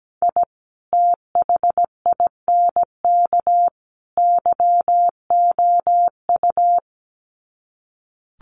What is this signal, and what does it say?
Morse code, heard as a keyed tone, about 17 wpm, 700 Hz. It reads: I THINK YOU
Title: Sets of 3 Words 17wpm